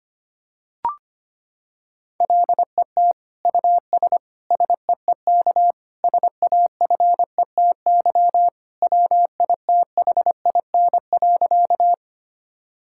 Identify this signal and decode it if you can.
Morse code, heard as a keyed tone, about 25 wpm, 700 Hz. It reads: LET US SEEK SAFETY WITHIN.
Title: Sentences from Top 1600 Words 25wpm